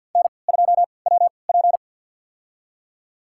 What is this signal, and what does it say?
Morse code, heard as a keyed tone, about 47 wpm, spaced as for 28 wpm, 700 Hz. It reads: N2WP
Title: USA Call Signs - 4 Characters - 3x Character Spacing - 50wpm